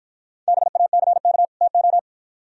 Morse code, heard as a keyed tone, about 51 wpm, 700 Hz. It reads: BACK TO